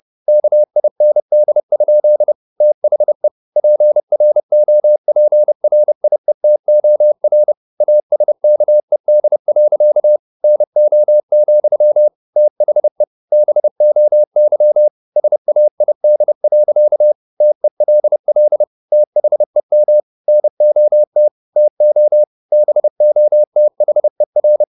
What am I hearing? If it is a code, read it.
Morse code, heard as a keyed tone, about 30 wpm, 600 Hz. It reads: KIND? THE PROPRIETOR ASKED. NO, THE BOY SAID. TELL THEM NOT TO BOTHER